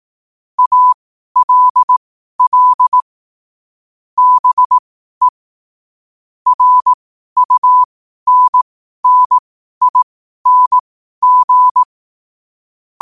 Morse code, heard as a keyed tone, about 18 wpm, spaced as for 12 wpm, 1000 Hz. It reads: ALL BE RUNNING
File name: qotdfiles / 12wpm